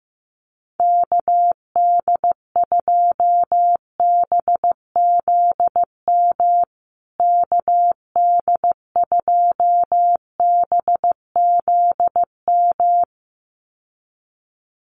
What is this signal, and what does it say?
Morse code, heard as a keyed tone, about 15 wpm, 700 Hz. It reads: KD2BZM KD2BZM
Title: USA Call Signs Repeated 2x 15wpm